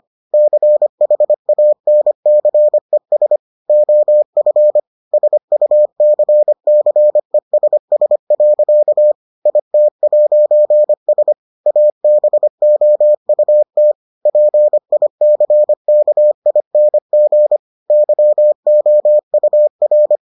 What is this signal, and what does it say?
Morse code, heard as a keyed tone, about 25 wpm, 600 Hz. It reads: CHANCES OF SUCCESS. IT'S ABOUT PICKING YOUR